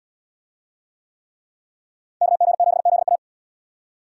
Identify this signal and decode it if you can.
Morse code, heard as a keyed tone, about 70 wpm, 700 Hz. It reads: KG7ZA